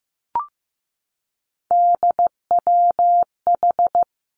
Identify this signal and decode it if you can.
Morse code, heard as a keyed tone, about 15 wpm, 700 Hz. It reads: DWH
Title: Three Challenging Letter-Number Combinations 15wpm